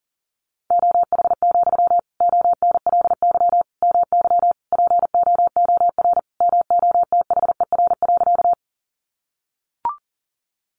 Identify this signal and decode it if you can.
Morse code, heard as a keyed tone, about 40 wpm, 700 Hz. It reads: OH, ONLY MY POOR MOTHER.
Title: Sentences from Top 1000 Words 40wpm